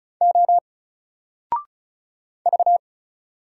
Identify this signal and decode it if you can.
Morse code, heard as a keyed tone, about 34 wpm, 700 Hz. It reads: O V
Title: Single Letter-Number 35wpm